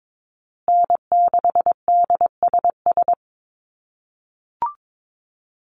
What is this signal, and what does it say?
Morse code, heard as a keyed tone, about 22 wpm, 700 Hz. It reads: N6DSS
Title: USA Call Signs Repeated 2x 22wpm